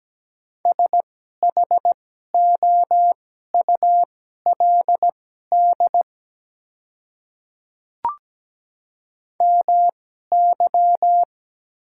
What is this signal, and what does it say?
Morse code, heard as a keyed tone, about 17 wpm, spaced as for 12 wpm, 700 Hz. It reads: SHOULD MY
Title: 2x Character Spacing - Top 300 Words - 17wpm